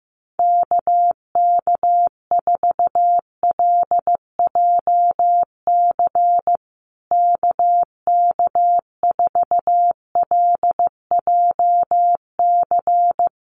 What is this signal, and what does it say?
Morse code, heard as a keyed tone, about 15 wpm, 700 Hz. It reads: KK4LJC KK4LJC
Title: USA Call Signs Repeated 3x 15wpm